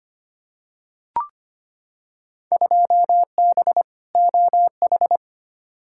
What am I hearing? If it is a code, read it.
Morse code, heard as a keyed tone, about 25 wpm, 700 Hz. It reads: 2B OH
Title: Field Day Exchanges Encore - 25wpm